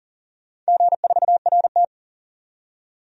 Morse code, heard as a keyed tone, about 40 wpm, 700 Hz. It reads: G4RT
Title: DXCC-100 Call Signs - 4 Characters - 40wpm